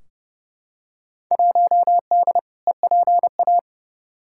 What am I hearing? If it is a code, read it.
Morse code, heard as a keyed tone, about 31 wpm, 700 Hz. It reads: 1D EPA
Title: Field Day Exchanges - 30wpm